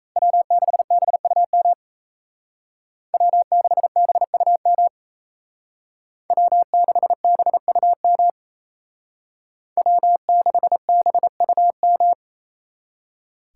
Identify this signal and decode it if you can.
Morse code, heard as a keyed tone, about 32 wpm, 700 Hz. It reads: W6BUM W6BUM W6BUM W6BUM
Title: Speed Racing - USA Call Signs 28wpm